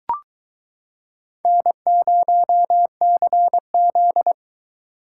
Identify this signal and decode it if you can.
Morse code, heard as a keyed tone, about 23 wpm, 700 Hz. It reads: N0CZ